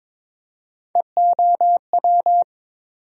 Morse code, heard as a keyed tone, about 22 wpm, 700 Hz. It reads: EOW
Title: USA Call Sign Postfixes - 3 Letters 22wpm